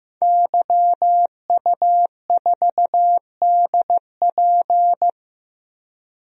Morse code, heard as a keyed tone, about 15 wpm, 700 Hz. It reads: YU4DP
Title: DXCC-100 Call Signs - 5 Characters - Repeated 2x - 15wpm